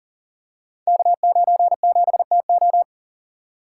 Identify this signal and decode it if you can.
Morse code, heard as a keyed tone, about 42 wpm, 700 Hz. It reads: K97TO
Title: Code Groups - 5 Letter:Numbers - 40wpm